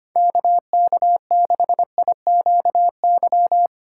Morse code, heard as a keyed tone, about 25 wpm, 700 Hz. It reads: KK6IQY
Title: USA Call Signs - 6 Characters - 3x Character Spacing - 25wpm